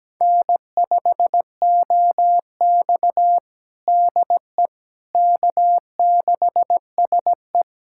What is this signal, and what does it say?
Morse code, heard as a keyed tone, about 17 wpm, 700 Hz. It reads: N5OX DE K6SE
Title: USA Callsign DE Callsign - 4 Characters - 17wpm